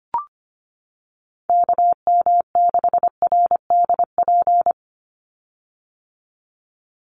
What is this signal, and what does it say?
Morse code, heard as a keyed tone, about 25 wpm, 700 Hz. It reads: KM6RDP
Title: USA Call Signs 25wpm